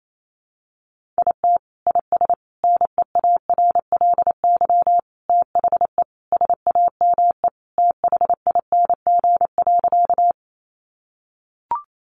Morse code, heard as a keyed tone, about 28 wpm, 700 Hz. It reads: IT IS NEARLY THE SAME THING.
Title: Sentences from Top 1300 Words 28wpm